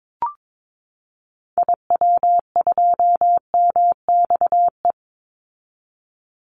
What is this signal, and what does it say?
Morse code, heard as a keyed tone, about 22 wpm, 700 Hz. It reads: IW2MXE
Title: Call Signs Repeated 2x - 22wpm